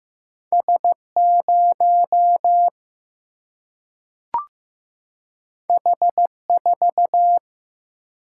Morse code, heard as a keyed tone, about 15 wpm, 700 Hz. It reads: S 0 H 4
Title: Single-Letter Single-Number 15wpm